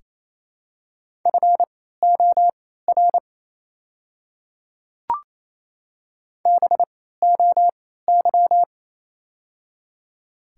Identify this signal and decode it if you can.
Morse code, heard as a keyed tone, about 28 wpm, spaced as for 16 wpm, 700 Hz. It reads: FOR BOY
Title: 3x Character Spacing - Top 500 Words - 28wpm